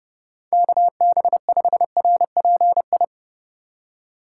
Morse code, heard as a keyed tone, about 35 wpm, 700 Hz. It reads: KB5RPI